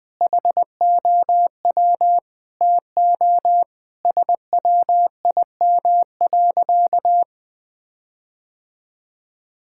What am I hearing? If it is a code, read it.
Morse code, heard as a keyed tone, about 20 wpm, 700 Hz. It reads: HOW TO SWIM.
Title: Sentences with 4-Word Phrases - Spoken Beginning - 20wpm